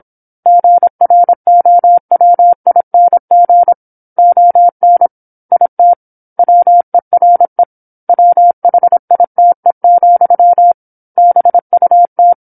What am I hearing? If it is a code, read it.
Morse code, heard as a keyed tone, about 27 wpm, 700 Hz. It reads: GROWING ON IT WERE WHITE, BUT
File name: 08-Chapter-08 26 WPM